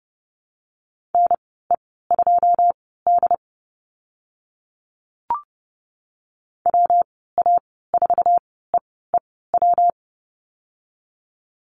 Morse code, heard as a keyed tone, about 30 wpm, spaced as for 17 wpm, 700 Hz. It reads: NE2D WA4EEW
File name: Lesson 19g - Letter w - Call Signs 30wpm